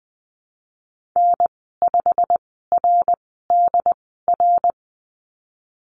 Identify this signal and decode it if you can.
Morse code, heard as a keyed tone, about 20 wpm, spaced as for 15 wpm, 700 Hz. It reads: N5RDR
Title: 088 Lesson 15g - Number 5 - Call Signs 20wpm